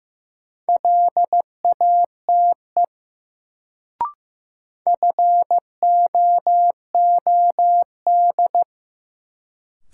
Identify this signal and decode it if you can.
Morse code, heard as a keyed tone, about 15 wpm, 700 Hz. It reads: LATE FOOD
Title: Sentences from Top 300 Words - New Word Drill - 15wpm